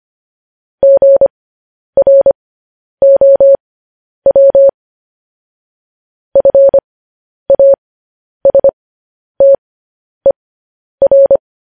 Morse code, heard as a keyed tone, about 25 wpm, spaced as for 10 wpm, 550 Hz. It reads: GROW FASTER
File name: podcast / News headlines-1x-25x10